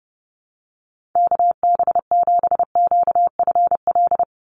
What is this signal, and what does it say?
Morse code, heard as a keyed tone, about 30 wpm, 700 Hz. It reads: KB7QFL
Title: USA Call Signs 30wpm